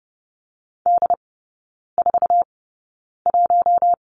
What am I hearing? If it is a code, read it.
Morse code, heard as a keyed tone, about 30 wpm, spaced as for 9 wpm, 700 Hz. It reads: D41
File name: Lesson 18e - Letter m - Three Characters 30wpm